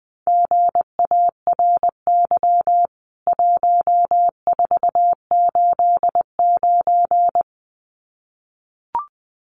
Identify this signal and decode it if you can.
Morse code, heard as a keyed tone, about 20 wpm, 700 Hz. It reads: GARY 1489
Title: CWT Exchange 20wpm